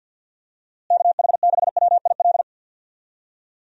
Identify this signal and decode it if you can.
Morse code, heard as a keyed tone, about 52 wpm, 700 Hz. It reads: KH6WIL
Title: USA Call Signs - 6 Characters - Repeated 3x - 50wpm